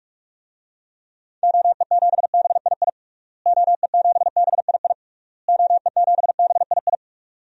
Morse code, heard as a keyed tone, about 44 wpm, 700 Hz. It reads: OE7BII OE7BII OE7BII
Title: DXCC-100 Call Signs - 6 Characters - Repeated 3x - 45wpm